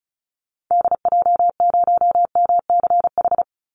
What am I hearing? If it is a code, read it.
Morse code, heard as a keyed tone, about 34 wpm, 700 Hz. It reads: DJ0MCH
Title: Call Signs Repeated 2x - 35wpm